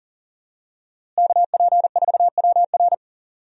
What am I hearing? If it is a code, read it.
Morse code, heard as a keyed tone, about 40 wpm, 700 Hz. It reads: KP4WR
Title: USA Call Signs - 5 Characters - Repeated 2x - 40wpm